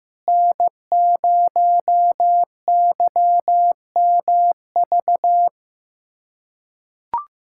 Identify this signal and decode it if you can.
Morse code, heard as a keyed tone, about 16 wpm, 700 Hz. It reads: N0YMV
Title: USA Callsign DE Callsign - 5 Characters - 15wpm